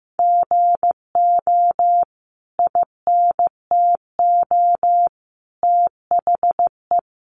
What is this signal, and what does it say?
Morse code, heard as a keyed tone, about 15 wpm, 700 Hz. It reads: GO INTO THE
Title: Sentences from Top 1400 Words 15wpm